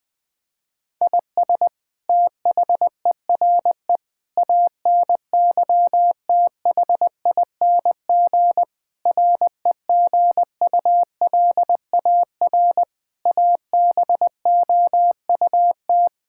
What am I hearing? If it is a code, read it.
Morse code, heard as a keyed tone, about 21 wpm, 700 Hz. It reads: IS THERE ANYTHING REGULAR ABOUT
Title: Sentences from Top 1000 Words - Review Part 6 - 20wpm